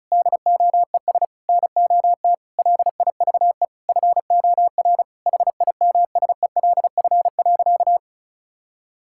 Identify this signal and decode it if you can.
Morse code, heard as a keyed tone, about 36 wpm, 700 Hz. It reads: DOES NOT LIVE FOR HIMSELF.
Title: Sentences from Top 1600 Words 35wpm